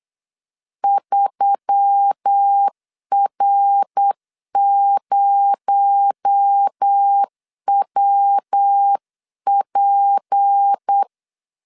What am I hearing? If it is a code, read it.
Morse code, heard as a keyed tone, about 8 wpm, 800 Hz. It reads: 3R0WP
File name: 8wpm